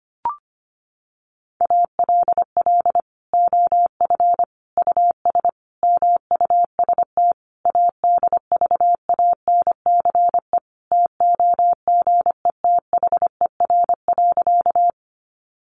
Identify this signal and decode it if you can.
Morse code, heard as a keyed tone, about 25 wpm, 700 Hz. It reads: ALL OF US MUST ADVANCE TOGETHER.
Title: Sentences from Top 1400 Words 25wpm